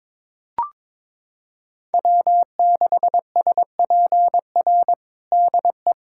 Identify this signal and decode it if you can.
Morse code, heard as a keyed tone, about 22 wpm, 700 Hz. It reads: W6SPR DE
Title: USA Callsign DE Callsign - 5 Characters - 22wpm